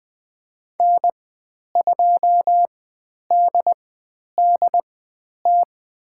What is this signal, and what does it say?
Morse code, heard as a keyed tone, about 20 wpm, spaced as for 10 wpm, 700 Hz. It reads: N2DDT
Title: 095 Lesson 16g - Letter c - Call Signs 20wpm